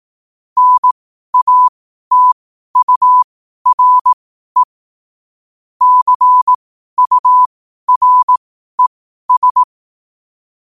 Morse code, heard as a keyed tone, about 18 wpm, spaced as for 12 wpm, 1000 Hz. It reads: NATURE CURES
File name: 12wpm